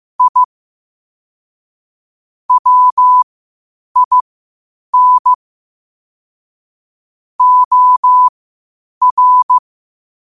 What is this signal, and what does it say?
Morse code, heard as a keyed tone, about 15 wpm, spaced as for 8 wpm, 1000 Hz. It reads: I WIN OR
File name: 8wpm